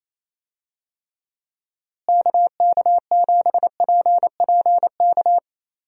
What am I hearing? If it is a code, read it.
Morse code, heard as a keyed tone, about 28 wpm, 700 Hz. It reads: KK7PPK
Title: USA Call Signs - 6 Characters - Repeated 2x - 28wpm